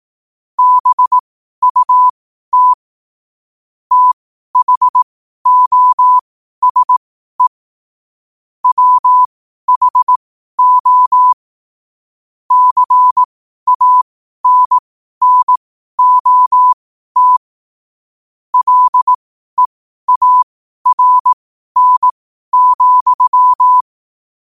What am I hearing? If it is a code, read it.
Morse code, heard as a keyed tone, about 18 wpm, spaced as for 12 wpm, 1000 Hz. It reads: BUT THOSE WHO CANNOT LEARN,
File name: 12wpm